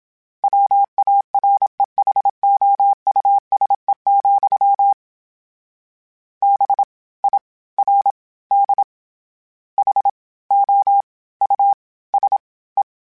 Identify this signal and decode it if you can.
Morse code, heard as a keyed tone, about 27 wpm, 800 Hz. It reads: WAREHOUSE, B I R D H O U S E
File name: General / General-mp3s / Houses @25